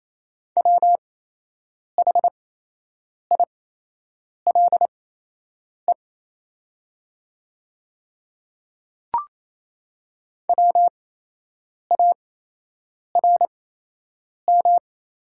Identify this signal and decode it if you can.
Morse code, heard as a keyed tone, about 27 wpm, spaced as for 8 wpm, 700 Hz. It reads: WHILE WARM
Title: 8x Character Spacing - Top 500 Words - 28wpm